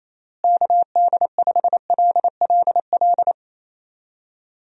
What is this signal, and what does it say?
Morse code, heard as a keyed tone, about 27 wpm, 700 Hz. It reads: KD5LLL
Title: USA Call Signs - 6 Characters - Repeated 3x - 28wpm